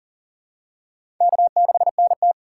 Morse code, heard as a keyed tone, about 40 wpm, 700 Hz. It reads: K6NT
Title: USA Call Signs - 4 Characters - 6x Character Spacing - 40wpm